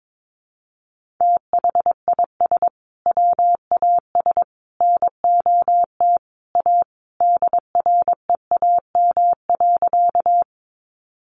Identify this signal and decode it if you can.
Morse code, heard as a keyed tone, about 22 wpm, 700 Hz. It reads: THIS WAS NOT A DREAM.
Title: Sentences from Top 1500 Words 22wpm